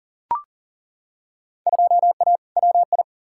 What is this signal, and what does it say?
Morse code, heard as a keyed tone, about 40 wpm, 700 Hz. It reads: 2A WI
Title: Field Day Exchanges - 40wpm